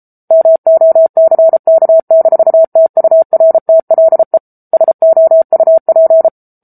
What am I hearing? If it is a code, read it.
Morse code, heard as a keyed tone, about 34 wpm, 650 Hz. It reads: MOCK-TURTLE SOUP
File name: Book Two / 07-Chapter 7 - 33 WPM